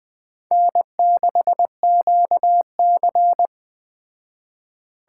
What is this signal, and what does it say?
Morse code, heard as a keyed tone, about 20 wpm, 700 Hz. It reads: N6QC
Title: Speed Racing - USA 1x2 Call Signs 17wpm